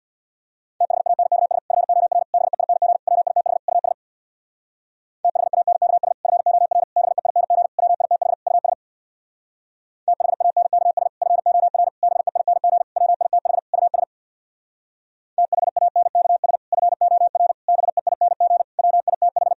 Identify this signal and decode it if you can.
Morse code, heard as a keyed tone, about 64 wpm, 700 Hz. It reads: THANKS FOR BEING WITH US THANKS FOR BEING WITH US THANKS FOR BEING WITH US THANKS FOR BEING WITH